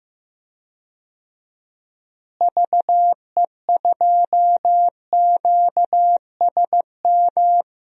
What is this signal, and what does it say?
Morse code, heard as a keyed tone, about 15 wpm, 700 Hz. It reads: VE2QSM
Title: DXCC-100 Call Signs - 6 Characters - Repeated 2x - 15wpm